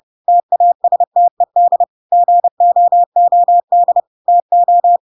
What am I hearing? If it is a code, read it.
Morse code, heard as a keyed tone, about 30 wpm, 700 Hz. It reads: TASTED GOOD TO
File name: Pinocchio / Pinocchio ch5 30 30 ik1hge